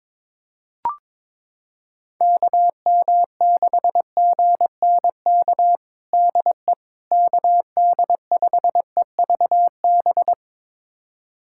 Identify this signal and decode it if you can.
Morse code, heard as a keyed tone, about 22 wpm, 700 Hz. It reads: KM6GNK DE KD5EVB
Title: USA Callsign DE Callsign - 6 Characters - 22wpm